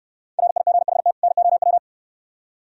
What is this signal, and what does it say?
Morse code, heard as a keyed tone, about 69 wpm, 700 Hz. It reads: RIGHT NOW